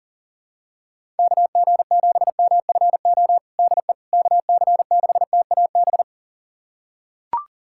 Morse code, heard as a keyed tone, about 40 wpm, 700 Hz. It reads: KG7MFO DE KC6TAB
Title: USA Callsign DE Callsign 40wpm